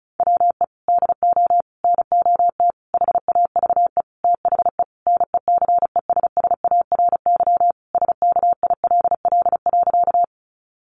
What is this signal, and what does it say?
Morse code, heard as a keyed tone, about 34 wpm, 700 Hz. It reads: WE DO NOT HAVE THE NECESSARY SKILL.
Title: Sentences from Top 1200 Words 35wpm